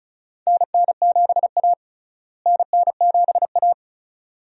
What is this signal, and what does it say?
Morse code, heard as a keyed tone, about 36 wpm, 700 Hz. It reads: NN7A NN7A
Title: USA Call Signs - 4 Characters - Repeated 2x - 35wpm